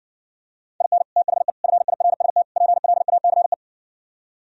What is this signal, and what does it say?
Morse code, heard as a keyed tone, about 60 wpm, 700 Hz. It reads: IN THE FIRST PLACE